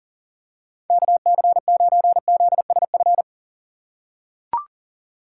Morse code, heard as a keyed tone, about 40 wpm, 700 Hz. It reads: KC9ZSF
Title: USA Call Signs - 3x Character Spacing - 40wpm